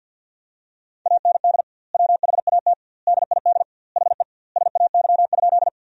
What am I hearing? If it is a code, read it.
Morse code, heard as a keyed tone, about 53 wpm, 700 Hz. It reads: AND WHAT DID HE SAY?
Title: 2x Word-Spacing - Sentences from Top 200 Words - 50wpm